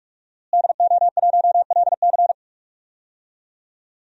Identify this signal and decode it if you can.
Morse code, heard as a keyed tone, about 47 wpm, 700 Hz. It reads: DO1LC
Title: DXCC-100 Call Signs - 5 Characters - Repeated 2x - 45wpm